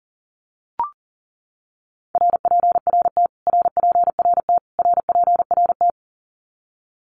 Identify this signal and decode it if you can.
Morse code, heard as a keyed tone, about 40 wpm, 700 Hz. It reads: RPRT RPRT RPRT
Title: Abbreviations Repeated 3x - 40wpm